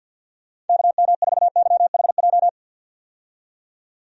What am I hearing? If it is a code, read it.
Morse code, heard as a keyed tone, about 50 wpm, 700 Hz. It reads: KM4YHJ
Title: USA Call Signs - 6 Characters - 50wpm